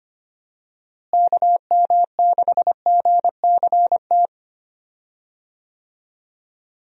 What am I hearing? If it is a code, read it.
Morse code, heard as a keyed tone, about 25 wpm, 700 Hz. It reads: KM6GCT
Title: USA Call Signs - 6 Characters - Repeated 2x - 25wpm